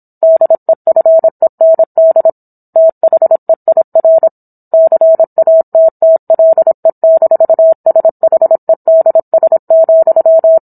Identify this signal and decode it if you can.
Morse code, heard as a keyed tone, about 27 wpm, 650 Hz. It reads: DEFEND THEIR CATTLE-SHEDS,